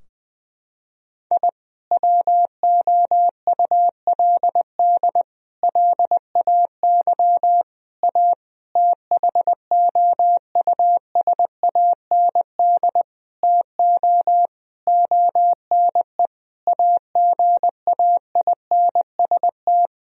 Morse code, heard as a keyed tone, about 20 wpm, 700 Hz. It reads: I WOULD LAY A THOUSAND TO ONE AGAINST
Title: Sentences from Top 1000 Words - Review Part 3 - 20wpm